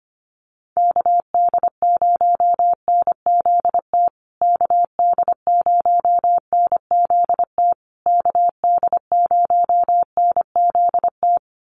Morse code, heard as a keyed tone, about 25 wpm, 700 Hz. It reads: KD0NZT KD0NZT KD0NZT
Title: USA Call Signs Repeated 3x 25wpm